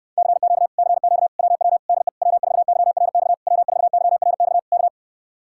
Morse code, heard as a keyed tone, about 63 wpm, 700 Hz. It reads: CQ CQ WW DE W4YRX W4YRX K